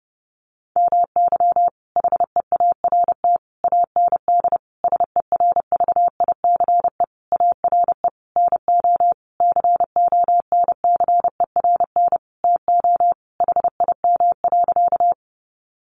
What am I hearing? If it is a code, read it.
Morse code, heard as a keyed tone, about 30 wpm, 700 Hz. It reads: MY HEART AND SERVICE ARE NO CONCERN TO HIM.
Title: Sentences from Top 1200 Words 30wpm